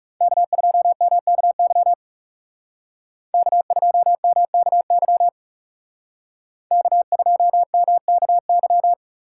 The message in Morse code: K2MKY K2MKY K2MKY

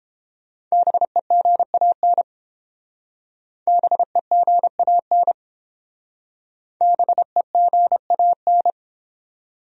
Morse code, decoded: BEGAN BEGAN BEGAN

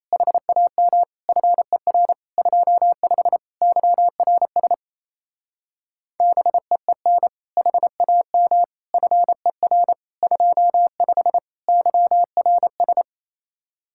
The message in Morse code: HAM FER 25 YRS BEEN HAM FER 25 YRS

